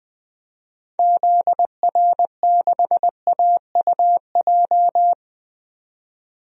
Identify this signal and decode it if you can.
Morse code, heard as a keyed tone, about 20 wpm, 700 Hz. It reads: ZR6AUJ
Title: DXCC-100 Call Signs - 6 Characters - 20wpm